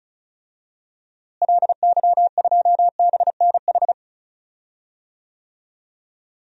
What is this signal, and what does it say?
Morse code, heard as a keyed tone, about 36 wpm, 700 Hz. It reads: LY2BNH
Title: DXCC-100 Call Signs - 6 Characters - Repeated 3x - 35wpm